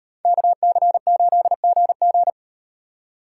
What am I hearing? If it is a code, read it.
Morse code, heard as a keyed tone, about 37 wpm, 700 Hz. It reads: KC8GG